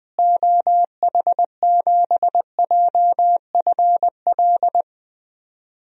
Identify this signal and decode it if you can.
Morse code, heard as a keyed tone, about 20 wpm, 700 Hz. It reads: OH7JFL
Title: DXCC-100 Call Signs - 6 Characters - Repeated 3x - 20wpm